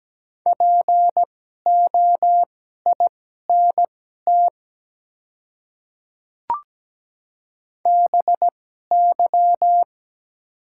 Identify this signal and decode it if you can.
Morse code, heard as a keyed tone, about 17 wpm, spaced as for 12 wpm, 700 Hz. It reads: POINT BY
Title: 2x Character Spacing - Top 300 Words - 17wpm